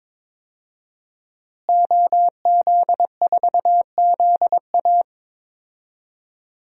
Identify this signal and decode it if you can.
Morse code, heard as a keyed tone, about 22 wpm, 700 Hz. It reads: OZ4ZA
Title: DXCC-100 Call Signs - 5 Characters - Repeated 2x - 22wpm